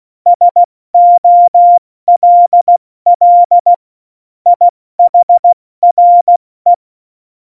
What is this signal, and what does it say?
Morse code, heard as a keyed tone, about 16 wpm, 700 Hz. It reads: SOLL IHRE
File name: Morsekurs / Witze / Witz 16. Januar 2018, 70 F80